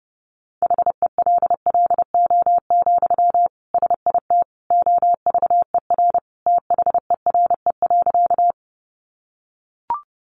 Morse code, heard as a keyed tone, about 30 wpm, 700 Hz. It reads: HELLO, SIT OVER THERE.
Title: Sentences from Top 1100 Words 30wpm